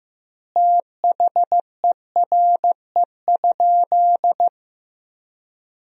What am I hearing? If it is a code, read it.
Morse code, heard as a keyed tone, about 15 wpm, 700 Hz. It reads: THERE?
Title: 2x Word-Spacing - Sentences from Top 100 Words - 15wpm